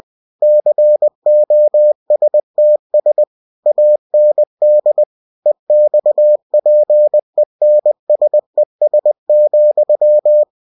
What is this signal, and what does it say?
Morse code, heard as a keyed tone, about 20 wpm, 600 Hz. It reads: COSTS AND EXPENSES,
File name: cw-books / winnie-the-pooh / winnie-20wpm-page-0120